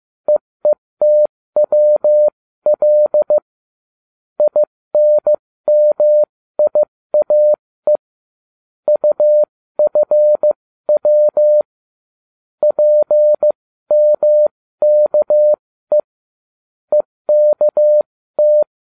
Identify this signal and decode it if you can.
Morse code, heard as a keyed tone, about 15 wpm, spaced as for 13 wpm, 600 Hz. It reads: EETWL INMIAE UFW PMKE EKT